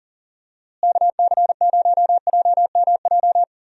Morse code, heard as a keyed tone, about 40 wpm, 700 Hz. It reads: KC0JMJ
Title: USA Call Signs - 6 Characters - Repeated 3x - 40wpm